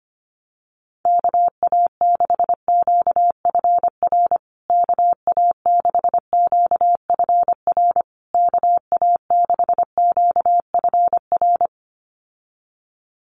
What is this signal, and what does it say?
Morse code, heard as a keyed tone, about 25 wpm, 700 Hz. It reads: KA6QFR KA6QFR KA6QFR
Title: USA Call Signs Repeated 3x 25wpm